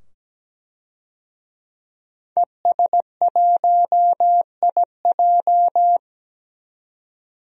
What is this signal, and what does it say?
Morse code, heard as a keyed tone, about 17 wpm, 700 Hz. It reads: ES1IJ
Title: DXCC-100 Call Signs - 5 Characters - 17wpm